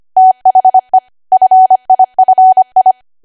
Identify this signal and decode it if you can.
Morse code, heard as a keyed tone, about 25 wpm, 750 Hz. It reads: THE FIFI
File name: cwa / W1AW-Level3 / 25WPM